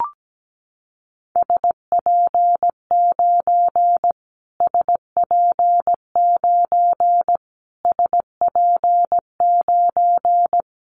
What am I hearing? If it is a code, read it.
Morse code, heard as a keyed tone, about 17 wpm, 700 Hz. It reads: SP9 SP9 SP9
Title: Callsign Prefixes for Top 25 DXCC - Repeated 3x - 17wpm